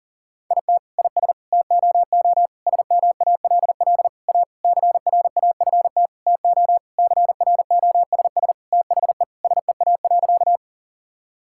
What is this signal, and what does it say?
Morse code, heard as a keyed tone, about 40 wpm, 700 Hz. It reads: IT IS TOO SMALL A CRAFT TO CROSS THE SEA.
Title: Sentences from Top 1000 Words - Review Part 6 - 40wpm